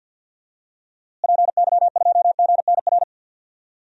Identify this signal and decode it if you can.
Morse code, heard as a keyed tone, about 50 wpm, 700 Hz. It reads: PY2GNR
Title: DXCC-100 Call Signs - 6 Characters - Repeated 2x - 50wpm